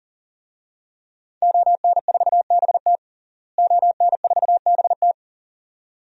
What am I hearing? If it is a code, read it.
Morse code, heard as a keyed tone, about 40 wpm, 700 Hz. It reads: ON4BT ON4BT
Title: DXCC-100 Call Signs - 5 Characters - Repeated 2x - 40wpm